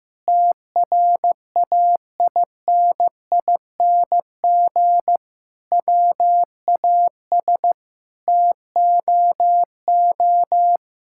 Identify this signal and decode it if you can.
Morse code, heard as a keyed tone, about 15 wpm, 700 Hz. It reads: TRAINING WAS TOO
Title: Sentences from Top 1000 Words - Review Part 5 - 15wpm